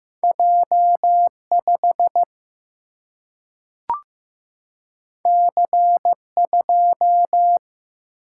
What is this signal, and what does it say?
Morse code, heard as a keyed tone, about 15 wpm, 700 Hz. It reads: J 5 C 2
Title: Single-Letter Single-Number 15wpm